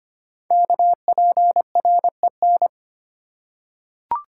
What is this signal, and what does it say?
Morse code, heard as a keyed tone, about 25 wpm, 700 Hz. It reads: KPREN
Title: Code Groups - 5 Letters - 25wpm